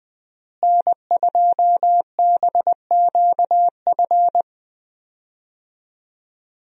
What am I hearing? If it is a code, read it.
Morse code, heard as a keyed tone, about 20 wpm, 700 Hz. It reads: N2BQF
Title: USA Call Signs - 5 Characters - Repeated 3x - 20wpm